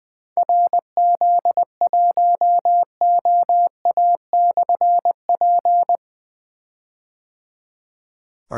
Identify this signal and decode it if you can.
Morse code, heard as a keyed tone, about 20 wpm, 700 Hz. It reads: RZ1OA/P
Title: Difficult Call Signs 20wpm